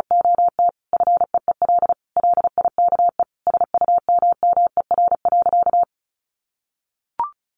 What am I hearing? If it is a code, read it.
Morse code, heard as a keyed tone, about 34 wpm, 700 Hz. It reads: OT FEEL LIKE SUMMER.
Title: Sentences from Top 1100 Words 35wpm